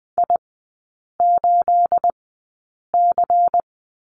Morse code, heard as a keyed tone, about 20 wpm, spaced as for 8 wpm, 690 Hz. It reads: I8C
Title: 212 Lesson 34e - Number 8 - Three Characters 20wpm